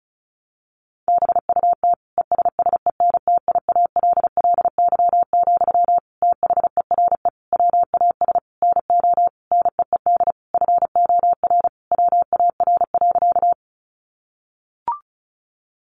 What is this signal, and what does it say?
Morse code, heard as a keyed tone, about 34 wpm, 700 Hz. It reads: BUT ESSENTIALLY, THERE WAS NO NEED FOR WAR.
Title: Sentences from Top 1300 Words 35wpm